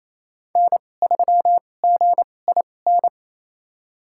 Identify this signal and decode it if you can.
Morse code, heard as a keyed tone, about 28 wpm, spaced as for 20 wpm, 700 Hz. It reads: N3GIN
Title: USA Call Signs - 5 Characters - 2x Character Spacing - 28wpm